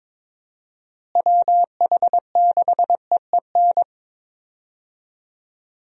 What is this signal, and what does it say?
Morse code, heard as a keyed tone, about 22 wpm, 700 Hz. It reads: WH6EEN
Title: USA Call Signs - 6 Characters - Repeated 2x - 22wpm